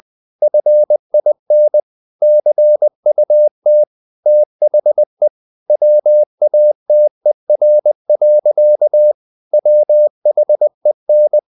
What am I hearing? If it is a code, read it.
Morse code, heard as a keyed tone, about 20 wpm, 600 Hz. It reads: FIN CUT THE WATER. WHEN
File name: the old man and the sea part 2 20wpm